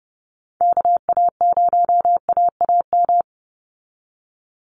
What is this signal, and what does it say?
Morse code, heard as a keyed tone, about 30 wpm, 700 Hz. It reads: KA0AAM
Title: Call Signs Repeated 2x - 30wpm